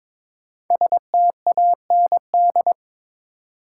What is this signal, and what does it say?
Morse code, heard as a keyed tone, about 22 wpm, 690 Hz. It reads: STAND
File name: 5 Letter Words - First 3 Letters Given - 22wpm